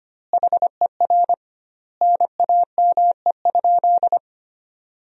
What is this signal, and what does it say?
Morse code, heard as a keyed tone, about 25 wpm, 700 Hz. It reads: HER NAME?
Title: 2x Word-Spacing - Sentences from Top 200 Words - 25wpm